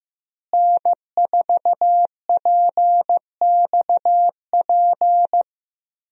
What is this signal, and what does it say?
Morse code, heard as a keyed tone, about 15 wpm, 700 Hz. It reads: N4PXP
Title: USA Call Signs - 5 Characters - Repeated 3x - 15wpm